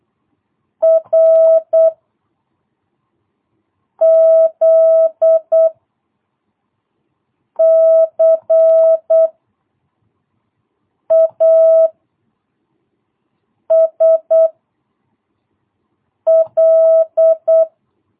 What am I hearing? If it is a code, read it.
Morse code, heard as a keyed tone, about 8 wpm, 660 Hz. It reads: R Z C A S L
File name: jh8cbh / 2022cw